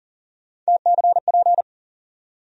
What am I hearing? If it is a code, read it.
Morse code, heard as a keyed tone, about 40 wpm, 700 Hz. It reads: TCP
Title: USA Call Sign Postfixes - 3 Letters 40wpm